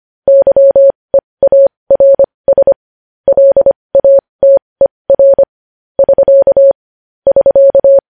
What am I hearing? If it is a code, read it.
Morse code, heard as a keyed tone, about 25 wpm, spaced as for 20 wpm, 550 Hz. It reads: YEARS LATER <SK> <SK>
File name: podcast / News headlines-1x-25x20